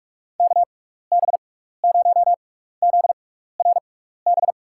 Morse code, heard as a keyed tone, about 46 wpm, spaced as for 15 wpm, 700 Hz. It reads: KB0ZRB
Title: USA Call Signs - 6 Characters - 6x Character Spacing - 45wpm